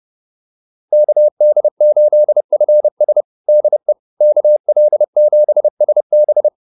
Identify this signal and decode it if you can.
Morse code, heard as a keyed tone, about 30 wpm, 600 Hz. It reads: KD8FS DE KL7SB